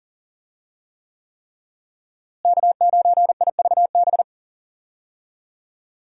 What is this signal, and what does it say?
Morse code, heard as a keyed tone, about 40 wpm, 700 Hz. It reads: K9IVB
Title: USA Call Signs - 5 Characters - Repeated 3x - 40wpm